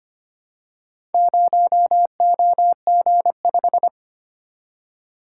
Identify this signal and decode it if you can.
Morse code, heard as a keyed tone, about 25 wpm, 700 Hz. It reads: 0OG5
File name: practice / 25wpm / Four Letter-Number Combinations - 25wpm